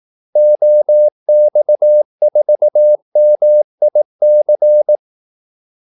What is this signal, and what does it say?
Morse code, heard as a keyed tone, about 18 wpm, 600 Hz. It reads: OX4MIC
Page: Kaldesignaler 18 wpm